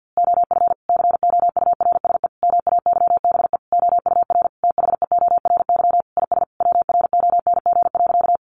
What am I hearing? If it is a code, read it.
Morse code, heard as a keyed tone, about 50 wpm, 700 Hz. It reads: OF COURSE MAYBE OUR THEORY IS WRONG.
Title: A Sound Of Thunder 50wpm